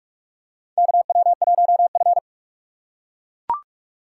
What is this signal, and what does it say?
Morse code, heard as a keyed tone, about 44 wpm, 690 Hz. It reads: KW1F
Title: USA Call Signs - 4 Characters - 2x Character Spacing - 45wpm